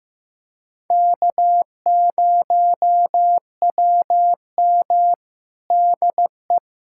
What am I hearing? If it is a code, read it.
Morse code, heard as a keyed tone, about 15 wpm, 700 Hz. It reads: K0WM DE
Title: USA Callsign DE Callsign - 4 Characters - 15wpm